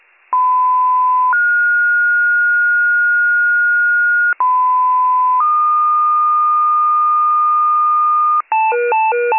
Two-tone page, heard as a one-second tone, then a three-second tone, about 1010 then 1430 Hz, about 1010 then 1220 Hz, with noise. It is heard from 0.3 to 8.4 s.